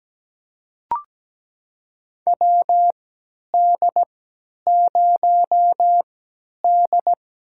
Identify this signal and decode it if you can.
Morse code, heard as a keyed tone, about 17 wpm, spaced as for 10 wpm, 700 Hz. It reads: WD0D
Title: USA Call Signs - 3x Character Spacing - 17wpm